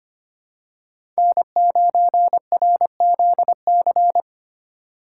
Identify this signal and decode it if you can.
Morse code, heard as a keyed tone, about 25 wpm, 700 Hz. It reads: N9RZC
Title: USA Call Signs - 8x Character Spacing - 25wpm